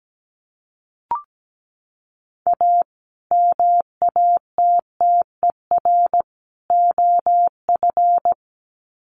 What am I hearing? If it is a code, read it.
Morse code, heard as a keyed tone, about 17 wpm, 700 Hz. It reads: A MATTER OF
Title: Sets of 3 Words 17wpm